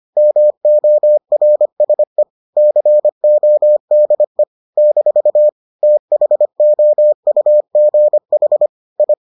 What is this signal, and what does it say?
Morse code, heard as a keyed tone, about 25 wpm, 600 Hz. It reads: MORSE CODE - THOUGH I